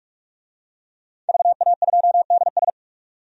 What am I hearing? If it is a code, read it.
Morse code, heard as a keyed tone, about 46 wpm, 700 Hz. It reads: VA2DS
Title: DXCC-100 Call Signs - 5 Characters - 45wpm